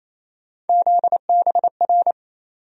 Morse code, heard as a keyed tone, about 28 wpm, 700 Hz. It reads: ZBR
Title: USA Call Sign Postfixes - 3 Letters 28wpm